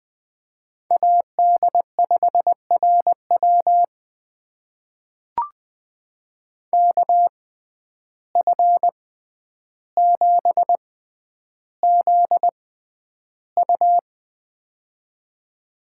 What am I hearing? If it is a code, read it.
Morse code, heard as a keyed tone, about 20 wpm, spaced as for 10 wpm, 700 Hz. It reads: AD5RW KF7ZU
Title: USA Call Signs - 5 Characters - 6x Character Spacing - 20wpm